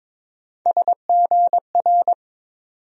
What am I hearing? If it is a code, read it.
Morse code, heard as a keyed tone, about 22 wpm, 700 Hz. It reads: SGR
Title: USA Call Sign Postfixes - 3 Letters 22wpm